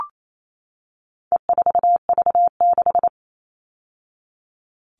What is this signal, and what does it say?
Morse code, heard as a keyed tone, about 28 wpm, 700 Hz. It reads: E4V6
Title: All Cut Numbers - 4-Digit 28wpm